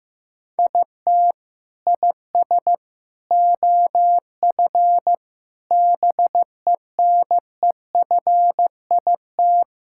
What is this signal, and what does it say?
Morse code, heard as a keyed tone, about 15 wpm, 700 Hz. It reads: IT IS OF BENEFIT
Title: Sentences from Top 1000 Words - Review Part 6 - 15wpm